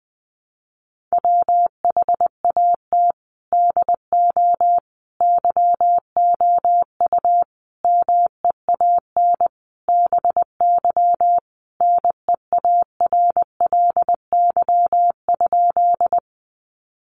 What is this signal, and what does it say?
Morse code, heard as a keyed tone, about 20 wpm, 700 Hz. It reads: WHAT DO YOU MEAN BY NEARLY?
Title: Sentences from Top 1300 Words 20wpm